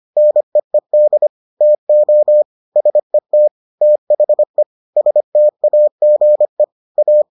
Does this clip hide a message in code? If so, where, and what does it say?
Yes: Morse code from 0.2 to 7.2 s, heard as a keyed tone, about 25 wpm, 600 Hz. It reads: NEED TO SET THE STAGE A